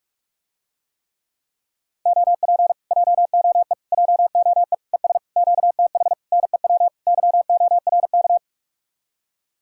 Morse code, heard as a keyed tone, about 49 wpm, 700 Hz. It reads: OP JOE JOE ES QTH NEW YORK